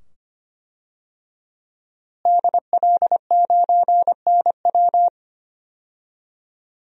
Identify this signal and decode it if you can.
Morse code, heard as a keyed tone, about 25 wpm, 700 Hz. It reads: DL9NW
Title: DXCC-100 Call Signs - 5 Characters - Repeated 2x - 25wpm